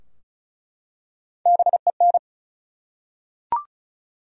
Morse code, heard as a keyed tone, about 36 wpm, 700 Hz. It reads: BEN
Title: USA Common Names - Max 4 Letters - 35wpm